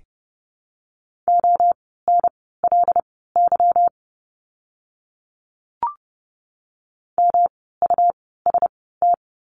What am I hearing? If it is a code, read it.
Morse code, heard as a keyed tone, about 30 wpm, spaced as for 17 wpm, 700 Hz. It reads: ONLY MUST
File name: course / 30:10wpm / 237 Lesson 38f - Letter q - Words 30wpm